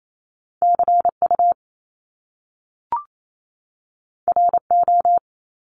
Morse code, heard as a keyed tone, about 28 wpm, 700 Hz. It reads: CU RO
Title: Top Two Letter Combinations in Words 28wpm